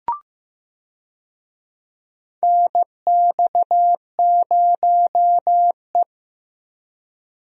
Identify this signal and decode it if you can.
Morse code, heard as a keyed tone, about 15 wpm, 700 Hz. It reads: NX0E
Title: USA Call Signs - 4 Characters - 15wpm